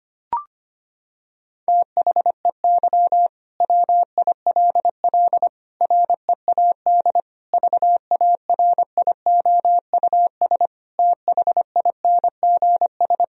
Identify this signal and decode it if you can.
Morse code, heard as a keyed tone, about 25 wpm, 700 Hz. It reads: THEY WILL READ VARIOUS THINGS
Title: Sentences from Top 1000 Words - Review Part 4 - 25wpm